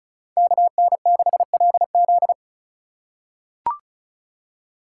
Morse code, heard as a keyed tone, about 34 wpm, 700 Hz. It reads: KN6LZ
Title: USA Call Signs - 4x Character Spacing - 35wpm